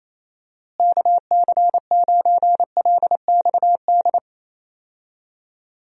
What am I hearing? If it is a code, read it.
Morse code, heard as a keyed tone, about 28 wpm, 700 Hz. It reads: KC9LXD
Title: USA Call Signs - 6 Characters - Repeated 3x - 28wpm